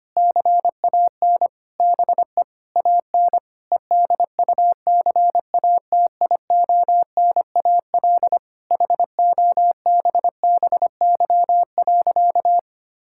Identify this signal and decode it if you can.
Morse code, heard as a keyed tone, about 25 wpm, 700 Hz. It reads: CAN BE AN EDUCATIONAL HOBBY.
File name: practice / 25wpm / Difficult Sentences with Hints - 25wpm